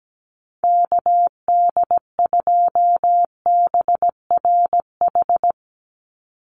Morse code, heard as a keyed tone, about 17 wpm, 700 Hz. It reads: KD2BRH
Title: USA Call Signs 17wpm